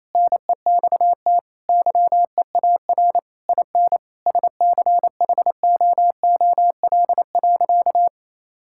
Morse code, heard as a keyed tone, about 28 wpm, 700 Hz. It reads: NEXT YEAR IN SCHOOL.
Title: Sentences from Top 1000 Words - Review Part 5 - 28wpm